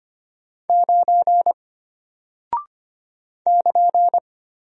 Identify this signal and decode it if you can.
Morse code, heard as a keyed tone, about 25 wpm, 700 Hz. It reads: 9 <KN>
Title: Single Character (Letter:Number:Punctuation:Prosign) Alt 25wpm